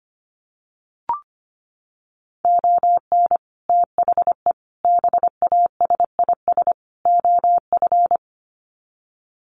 Morse code, heard as a keyed tone, about 25 wpm, 700 Hz. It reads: ON THE BASIS OF
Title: Sets of 4 Words 25wpm